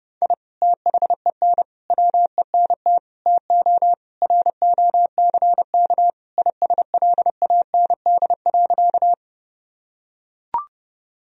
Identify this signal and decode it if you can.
Morse code, heard as a keyed tone, about 30 wpm, 700 Hz. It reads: I THEN WENT TO ROCK ISLAND.
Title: Sentences from Top 1000 Words - Review Part 3 - 30wpm